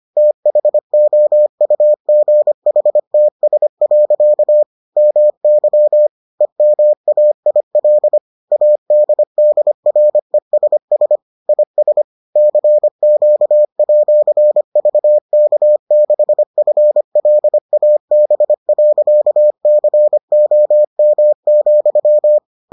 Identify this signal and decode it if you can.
Morse code, heard as a keyed tone, about 25 wpm, 600 Hz. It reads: THOUGHTS. MY EMAIL ADDRESS IS CQ@VK6FLAB.COM,